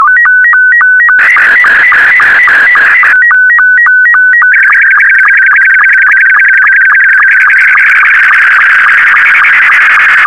SSTV (slow-scan television), Scottie 2.